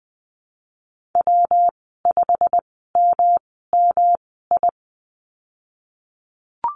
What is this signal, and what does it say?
Morse code, heard as a keyed tone, about 20 wpm, spaced as for 15 wpm, 700 Hz. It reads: W5MMI
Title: 238 Lesson 38g - Letter q - Call Signs 20wpm